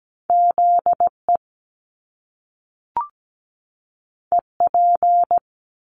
Morse code, heard as a keyed tone, about 17 wpm, 700 Hz. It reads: ZE EP
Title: Top Two Letter Combinations in Words 17wpm